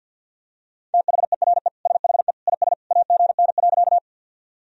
Morse code, heard as a keyed tone, about 46 wpm, 700 Hz. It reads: THERE SHE IS AGN.
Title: Sentences from Top 200 Words and Abbreviations 50wpm